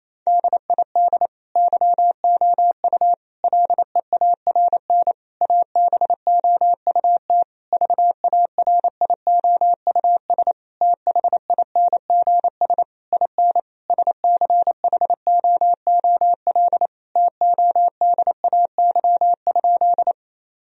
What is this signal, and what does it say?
Morse code, heard as a keyed tone, about 28 wpm, 700 Hz. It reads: DID YOU LEARN ABOUT VARIOUS THINGS IN SCHOOL TODAY?
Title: Sentences from Top 1000 Words - Review Part 4 - 28wpm